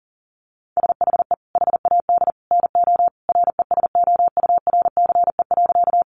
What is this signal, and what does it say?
Morse code, heard as a keyed tone, about 40 wpm, 700 Hz. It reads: SHE HAD NO RESOURCE.
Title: Sentences from Top 1300 Words 40wpm